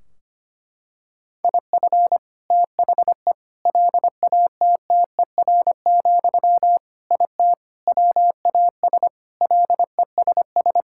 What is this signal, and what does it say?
Morse code, heard as a keyed tone, about 25 wpm, 700 Hz. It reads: IF THE LATTER, IT WAS LESS
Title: Sentences from Top 1600 Words 25wpm